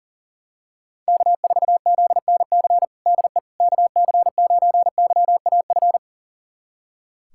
Morse code, heard as a keyed tone, about 37 wpm, 700 Hz. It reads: K4ZNC DE KC9YAF
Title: USA Callsign DE Callsign 40wpm